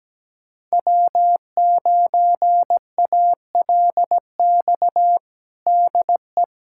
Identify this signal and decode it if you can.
Morse code, heard as a keyed tone, about 17 wpm, 700 Hz. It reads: W9ALX DE
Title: USA Callsign DE Callsign - 5 Characters - 17wpm